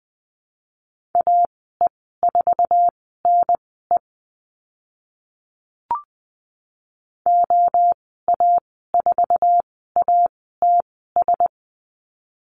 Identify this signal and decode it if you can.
Morse code, heard as a keyed tone, about 20 wpm, spaced as for 15 wpm, 700 Hz. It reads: AE4NE OA4ATS
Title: 062 Lesson 11g - Letter h - Call Signs 20wpm